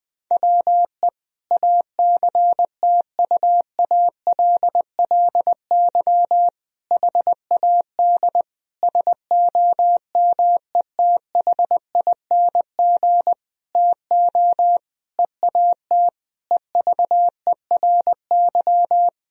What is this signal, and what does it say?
Morse code, heard as a keyed tone, about 21 wpm, 700 Hz. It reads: WE ACTUALLY HAD SOMETHING TO EAT EVERY
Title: Sentences from Top 1000 Words - Review Part 4 - 20wpm